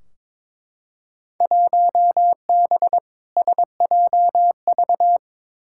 Morse code, heard as a keyed tone, about 22 wpm, 700 Hz. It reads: 1B SJV
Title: Field Day Exchanges - 22wpm